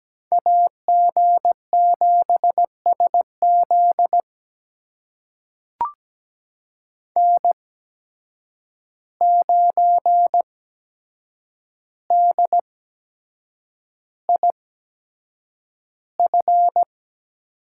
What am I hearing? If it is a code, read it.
Morse code, heard as a keyed tone, about 17 wpm, spaced as for 7 wpm, 700 Hz. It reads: AG7SZ N9DIF
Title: USA Call Signs - 5 Characters - 8x Character Spacing - 17wpm